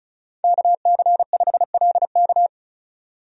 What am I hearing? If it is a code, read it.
Morse code, heard as a keyed tone, about 34 wpm, 700 Hz. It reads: KC5LK
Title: USA Call Signs - 5 Characters - 35wpm